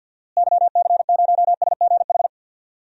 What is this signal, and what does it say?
Morse code, heard as a keyed tone, about 52 wpm, 700 Hz. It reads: YC0SGH
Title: DXCC-100 Call Signs - 6 Characters - Repeated 3x - 50wpm